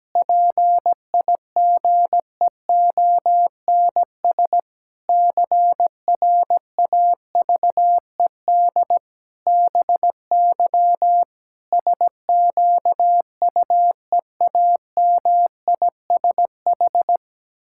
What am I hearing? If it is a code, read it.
Morse code, heard as a keyed tone, about 17 wpm, 700 Hz. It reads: PIGEONS CRAVED BY SQUEAMISH
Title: Pangram Challenge v2 - 17wpm